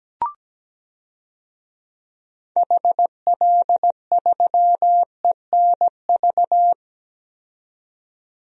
Morse code, heard as a keyed tone, about 17 wpm, 700 Hz. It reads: HL3ENV
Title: DXCC-100 Call Signs - 6 Characters - 17wpm